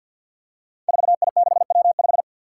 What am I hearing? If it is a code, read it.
Morse code, heard as a keyed tone, about 52 wpm, 700 Hz. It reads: 4I6W5
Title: Five Challenging Letter-Number Combinations 50wpm